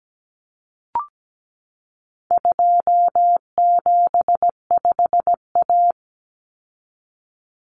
Morse code, heard as a keyed tone, about 17 wpm, 700 Hz. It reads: 275A
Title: Common Cut Numbers - 4-Digit 17wpm